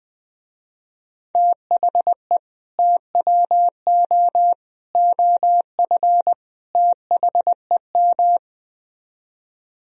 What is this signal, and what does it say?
Morse code, heard as a keyed tone, about 20 wpm, 700 Hz. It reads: THE TWO OF THEM
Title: Speed-Racing - Sets of 4 Words 17wpm